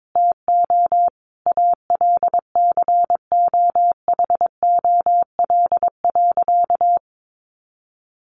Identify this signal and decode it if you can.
Morse code, heard as a keyed tone, about 22 wpm, 700 Hz. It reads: TO ALCOHOL.
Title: Sentences from Top 1400 Words 22wpm